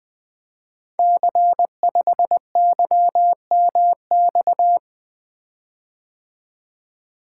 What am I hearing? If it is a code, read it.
Morse code, heard as a keyed tone, about 20 wpm, 700 Hz. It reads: C5YMX
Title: Code Groups - 5 Letter:Numbers - 20wpm